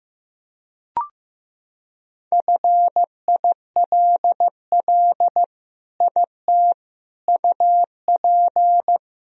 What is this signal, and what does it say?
Morse code, heard as a keyed tone, about 15 wpm, 700 Hz. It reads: FILL IT UP
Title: Sentences from Top 1000 Words - Review Part 3 - 15wpm